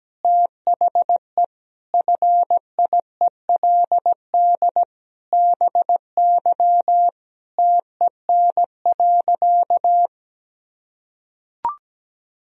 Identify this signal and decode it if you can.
Morse code, heard as a keyed tone, about 17 wpm, 700 Hz. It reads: THE FIELD BY TEN.
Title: Sentences from Top 1000 Words - Review Part 2 - 17wpm